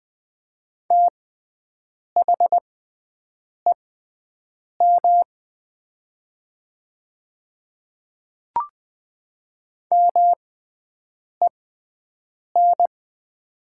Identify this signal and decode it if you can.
Morse code, heard as a keyed tone, about 20 wpm, spaced as for 7 wpm, 700 Hz. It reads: THEM MEN
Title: 6x Character Spacing - Top 200 Words - 20wpm